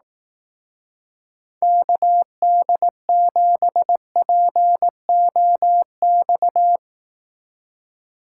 Morse code, heard as a keyed tone, about 18 wpm, 700 Hz. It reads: KD7POX
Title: Speed Racing - USA 2x3 Call Signs 15wpm